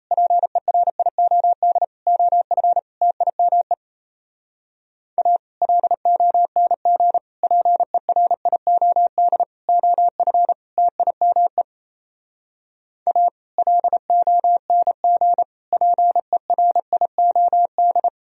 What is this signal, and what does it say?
Morse code, heard as a keyed tone, about 37 wpm, 700 Hz. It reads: PERIOD OF TIME A LONG PERIOD OF TIME A LONG PERIOD